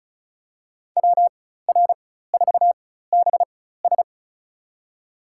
Morse code, heard as a keyed tone, about 34 wpm, spaced as for 16 wpm, 700 Hz. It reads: WR4BS
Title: USA Call Signs - 5 Characters - 4x Character Spacing - 35wpm